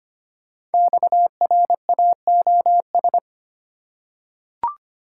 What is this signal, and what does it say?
Morse code, heard as a keyed tone, about 25 wpm, 700 Hz. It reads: XRAOS
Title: Code Groups - 5 Letters - 25wpm